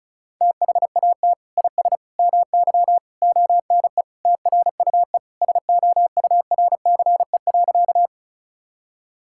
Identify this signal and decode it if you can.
Morse code, heard as a keyed tone, about 36 wpm, 700 Hz. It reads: THAT IS MY ONE TRUE SOURCE.
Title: Sentences from Top 1000 Words - Review Part 5 - 35wpm